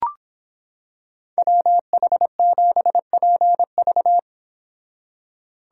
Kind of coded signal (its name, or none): Morse code